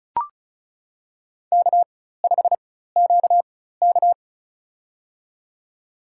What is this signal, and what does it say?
Morse code, heard as a keyed tone, about 36 wpm, spaced as for 16 wpm, 700 Hz. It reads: K5QK
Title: USA Call Signs - 4 Characters - 4x Character Spacing - 35wpm